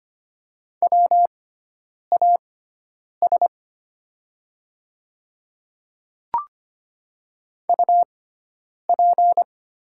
Morse code, heard as a keyed tone, about 25 wpm, spaced as for 9 wpm, 700 Hz. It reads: WAS UP
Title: 6x Character Spacing - Top 100 Words - 25wpm